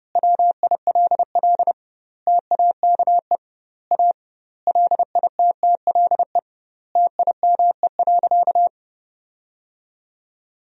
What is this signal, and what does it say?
Morse code, heard as a keyed tone, about 30 wpm, 700 Hz. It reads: WILL TAKE A LITTLE TIME.
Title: 2x Word-Spacing - Sentences from Top 200 Words - 30wpm